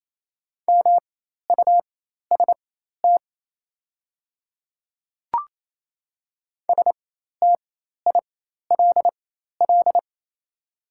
Morse code, heard as a keyed tone, about 27 wpm, spaced as for 13 wpm, 700 Hz. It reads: MUST STILL
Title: 4x Character Spacing - Top 500 Words - 28wpm